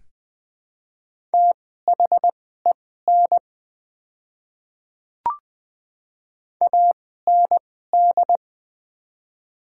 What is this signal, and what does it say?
Morse code, heard as a keyed tone, about 20 wpm, spaced as for 15 wpm, 700 Hz. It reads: THEN AND
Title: 2x Character Spacing - Top 100 Words - 20wpm